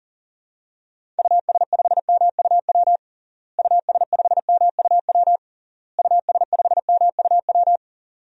Morse code, heard as a keyed tone, about 40 wpm, 700 Hz. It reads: US5MUW US5MUW US5MUW
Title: DXCC-100 Call Signs - 6 Characters - Repeated 3x - 40wpm